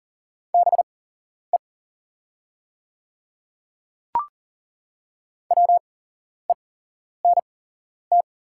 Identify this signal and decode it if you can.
Morse code, heard as a keyed tone, about 39 wpm, spaced as for 11 wpm, 700 Hz. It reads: BE WENT
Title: 8x Character Spacing - Top 300 Words - 40wpm